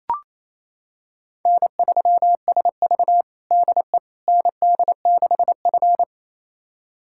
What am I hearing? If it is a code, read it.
Morse code, heard as a keyed tone, about 28 wpm, 700 Hz. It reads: N3SV DE ND6F
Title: USA Callsign DE Callsign - 4 Characters - 28wpm